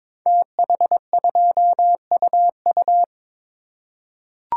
Morse code, heard as a keyed tone, about 22 wpm, 690 Hz. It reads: TH2UU
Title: Code Groups - 5 Letter:Numbers - 22wpm